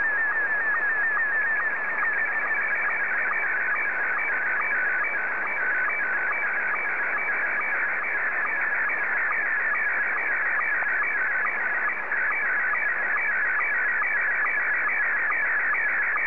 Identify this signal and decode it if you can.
SSTV (slow-scan television), Scottie 1.